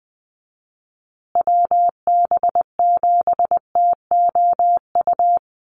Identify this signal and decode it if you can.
Morse code, heard as a keyed tone, about 20 wpm, 700 Hz. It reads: WB7TOU
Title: USA Call Signs Repeated 2x 20wpm